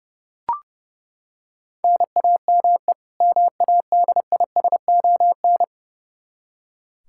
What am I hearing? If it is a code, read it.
Morse code, heard as a keyed tone, about 30 wpm, 700 Hz. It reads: NAME MADISON
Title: Practice QSO - Exchange 1 - Beginning - 30wpm